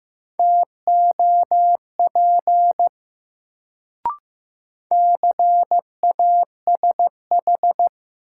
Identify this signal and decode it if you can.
Morse code, heard as a keyed tone, about 15 wpm, 700 Hz. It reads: TOP CASH
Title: Sentences from Top 700 Words - New Word Drill - 15wpm